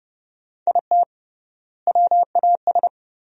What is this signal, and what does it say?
Morse code, heard as a keyed tone, about 32 wpm, 700 Hz. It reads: IT WAS
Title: 3x Word-Spacing - Sentences from Top 400 Words - 30wpm